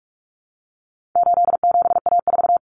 Morse code, heard as a keyed tone, about 44 wpm, 700 Hz. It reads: 87A4
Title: Common Cut Numbers - 4-Digit 45wpm